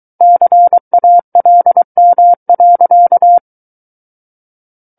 Morse code, heard as a keyed tone, about 23 wpm, 700 Hz. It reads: CALM.
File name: Book One / 14-Chapter 14 - 23 WPM